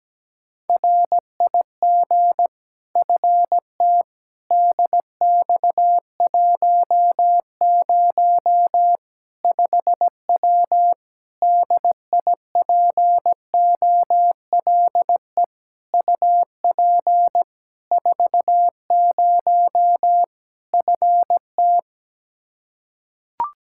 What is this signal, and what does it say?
Morse code, heard as a keyed tone, about 17 wpm, 700 Hz. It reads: RIG FT DX10 5W DIPOLE UP 40 FT
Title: Practice QSO - Exchange 2 - Info - 17wpm